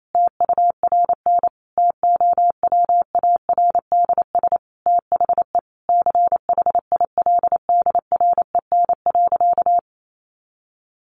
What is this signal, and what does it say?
Morse code, heard as a keyed tone, about 28 wpm, 700 Hz. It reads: TURN TOWARDS THE CHILDREN.
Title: Sentences from Top 1100 Words 28wpm